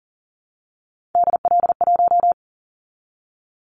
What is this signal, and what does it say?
Morse code, heard as a keyed tone, about 40 wpm, 700 Hz. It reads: DL1
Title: Callsign Prefixes for Top 25 DXCC - 40wpm